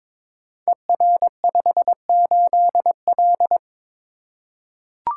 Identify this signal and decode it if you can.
Morse code, heard as a keyed tone, about 22 wpm, 700 Hz. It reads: ER58L
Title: Code Groups - 5 Letter:Numbers - 22wpm